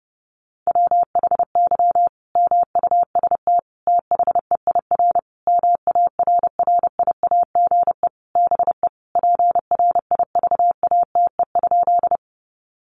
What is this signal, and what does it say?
Morse code, heard as a keyed tone, about 30 wpm, 700 Hz. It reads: WHY MUST THEIR MARRIAGE BE PRIVATE?
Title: Sentences from Top 1000 Words 30wpm